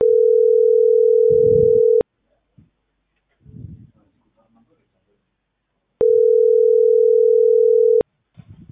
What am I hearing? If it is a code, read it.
Ringback tone (North America), 440 + 480 Hz.